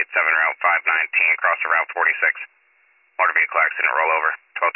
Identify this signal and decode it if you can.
Emergency Alert System (SAME) tones, heard as harsh buzzing bursts, with speech. It is heard from 0.6 to 3.8 s.